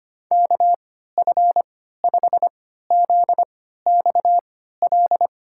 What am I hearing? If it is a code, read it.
Morse code, heard as a keyed tone, about 25 wpm, spaced as for 14 wpm, 700 Hz. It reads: KF5ZXL
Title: USA Call Signs - 6 Characters - 3x Character Spacing - 25wpm